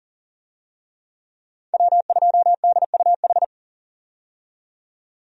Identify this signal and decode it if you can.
Morse code, heard as a keyed tone, about 37 wpm, 700 Hz. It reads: W2DUH
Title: USA Call Signs - 5 Characters - 40wpm